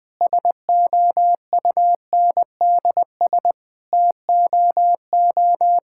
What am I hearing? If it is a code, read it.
Morse code, heard as a keyed tone, about 21 wpm, 700 Hz. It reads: SOUNDS TOO
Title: Sets of 9 Words - 20wpm